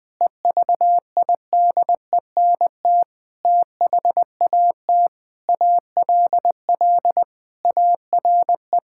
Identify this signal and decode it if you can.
Morse code, heard as a keyed tone, about 20 wpm, 700 Hz. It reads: EVIDENT THAT ALL ARE
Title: Sentences with 9-Word Phrases - Spoken Beginning - 20wpm